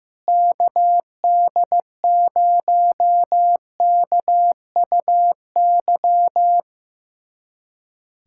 Morse code, heard as a keyed tone, about 16 wpm, 700 Hz. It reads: KD0KUY
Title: USA Callsign DE Callsign 15wpm